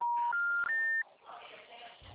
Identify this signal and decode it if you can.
Special information tone.